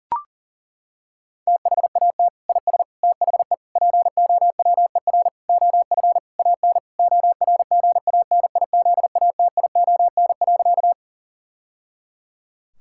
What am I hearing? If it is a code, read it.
Morse code, heard as a keyed tone, about 40 wpm, 700 Hz. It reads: THAT IS THE POWER OF AN ORGANIZATION.
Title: Sentences from Top 1000 Words - Review Part 5 - 40wpm